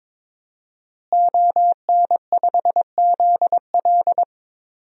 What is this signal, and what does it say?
Morse code, heard as a keyed tone, about 22 wpm, 700 Hz. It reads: ON5ZL
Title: DXCC-100 Call Signs - 5 Characters - Repeated 2x - 22wpm